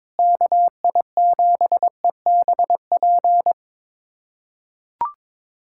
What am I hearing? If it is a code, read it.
Morse code, heard as a keyed tone, about 22 wpm, 700 Hz. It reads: KI7EBP
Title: USA Callsign DE Callsign - 6 Characters - 22wpm